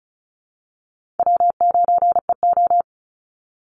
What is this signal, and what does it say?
Morse code, heard as a keyed tone, about 34 wpm, 700 Hz. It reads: W9EO
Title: Call Signs Repeated 3x - 35wpm